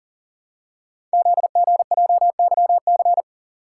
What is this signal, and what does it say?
Morse code, heard as a keyed tone, about 39 wpm, 690 Hz. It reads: ZGJYC
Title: Code Groups - 5 Letter:Numbers - 40wpm